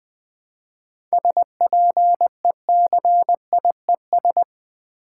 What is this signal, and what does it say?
Morse code, heard as a keyed tone, about 20 wpm, 700 Hz. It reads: SPECIES
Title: 7 Letter Words - First 3 Letters Given - 20wpm